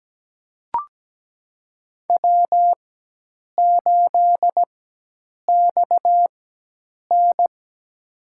Morse code, heard as a keyed tone, about 17 wpm, spaced as for 8 wpm, 700 Hz. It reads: W8XN
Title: USA Call Signs - 4 Characters - 4x Character Spacing - 17wpm